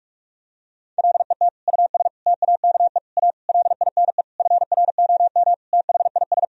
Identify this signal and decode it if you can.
Morse code, heard as a keyed tone, about 46 wpm, 700 Hz. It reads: LET US TAKE A LINE FROM THIS
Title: Sentences from Top 1000 Words - Review Part 1 - 45wpm